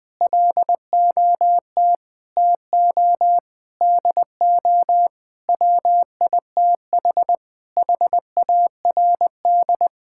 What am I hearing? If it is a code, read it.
Morse code, heard as a keyed tone, about 21 wpm, 700 Hz. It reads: LOT TO DO WITH HARD
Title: Sentences with 6-Word Phrases - 20wpm